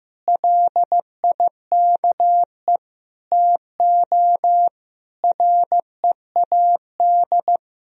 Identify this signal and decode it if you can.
Morse code, heard as a keyed tone, about 15 wpm, 700 Hz. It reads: LIKE TO READ
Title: Sentences from Top 500 Words and Abbreviations 15wpm